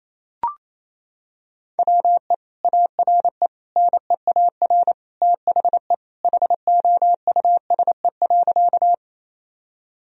Morse code, heard as a keyed tone, about 28 wpm, 700 Hz. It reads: WE ARE NEAR THE HOUSE.
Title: Sentences from Top 1000 Words - Review Part 1 - 28wpm